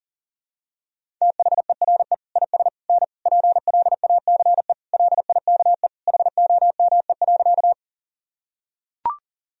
Morse code, heard as a keyed tone, about 40 wpm, 700 Hz. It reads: THERE IS N PLACE LIKE HOME.
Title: Sentences from Top 200 Words and Abbreviations 40wpm